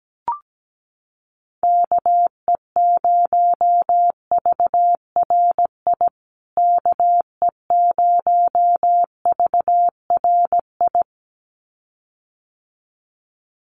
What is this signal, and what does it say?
Morse code, heard as a keyed tone, about 17 wpm, 700 Hz. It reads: KE0VRI KE0VRI
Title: USA Call Signs Repeated 2x 17wpm